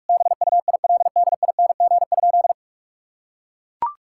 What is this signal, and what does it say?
Morse code, heard as a keyed tone, about 41 wpm, 700 Hz. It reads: BUILDING?
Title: Sentences from Top 1000 Words - Review Part 5 - 45wpm